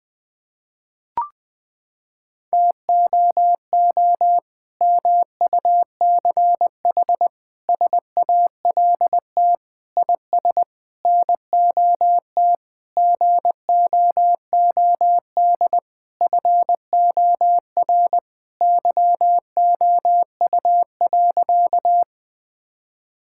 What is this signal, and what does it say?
Morse code, heard as a keyed tone, about 20 wpm, 700 Hz. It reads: TOO MUCH SALT IS NOT GOOD FOR YOU.
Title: Sentences from Top 1000 Words - Review Part 5 - 20wpm